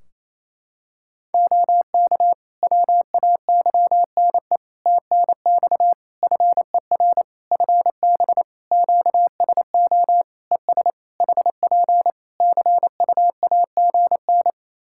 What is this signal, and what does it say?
Morse code, heard as a keyed tone, about 27 wpm, 700 Hz. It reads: OK WAYNE TNX FER FB QSO ES HP CUAGN
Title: Practice QSO - Exchange 4 - Ending - 28wpm